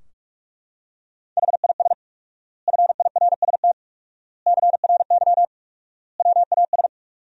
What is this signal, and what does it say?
Morse code, heard as a keyed tone, about 42 wpm, 700 Hz. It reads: HIS FIRST CRY WAS
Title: 4x Word-Spacing - Sentences from Top 500 Words - 45wpm